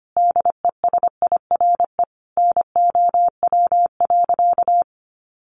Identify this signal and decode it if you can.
Morse code, heard as a keyed tone, about 25 wpm, 690 Hz. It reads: DESIRE NOW.
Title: Sentences from Top 1100 Words 25wpm